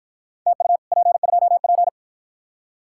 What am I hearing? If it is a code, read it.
Morse code, heard as a keyed tone, about 49 wpm, 700 Hz. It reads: TU P2P